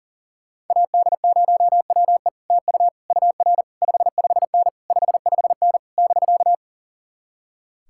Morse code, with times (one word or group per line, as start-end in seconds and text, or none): 0.7-2.3 AD0WE
2.5-2.9 TU
3.1-3.6 UR
3.8-4.7 55N
4.9-5.8 55N
6.0-6.5 <BK>